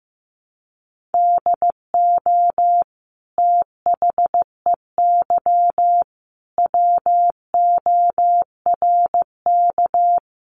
Morse code, heard as a keyed tone, about 15 wpm, 700 Hz. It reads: DO THEY WORK
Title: Sentences from Top 1300 Words 15wpm